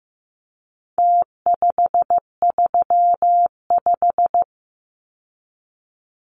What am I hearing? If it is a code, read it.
Morse code, heard as a keyed tone, about 15 wpm, 700 Hz. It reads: T535
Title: Numbers - Contest Style - 4-Digit 15wpm